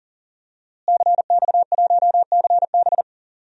Morse code, heard as a keyed tone, about 41 wpm, 700 Hz. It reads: CX1CB
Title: DXCC-100 Call Signs - 5 Characters - Repeated 3x - 40wpm